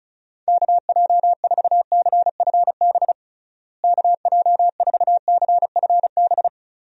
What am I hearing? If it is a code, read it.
Morse code, heard as a keyed tone, about 34 wpm, 700 Hz. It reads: KJ4CFB KJ4CFB
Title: USA Call Signs - 6 Characters - Repeated 2x - 35wpm